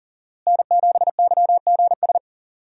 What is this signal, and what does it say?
Morse code, heard as a keyed tone, about 40 wpm, 700 Hz. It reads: N7YGS
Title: USA Call Signs - 5 Characters - Repeated 2x - 40wpm